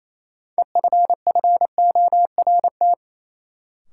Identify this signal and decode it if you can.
Morse code, heard as a keyed tone, about 28 wpm, 700 Hz. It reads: EFFORT
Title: Sentences from Top 1200 Words - New Word Drill - 28wpm